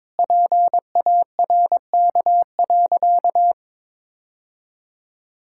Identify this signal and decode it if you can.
Morse code, heard as a keyed tone, about 22 wpm, 700 Hz. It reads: PARK.
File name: practice / 22wpm / Sentences with 3-Word Phrases - 22wpm